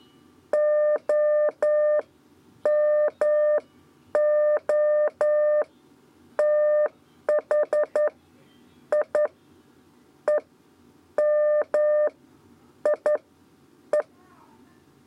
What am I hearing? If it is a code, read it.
Morse code, heard as a keyed tone, about 11 wpm, spaced as for 8 wpm, 600 Hz. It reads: OMOTHI EMIE
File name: Lezione trasmessa 2 lez 2 exe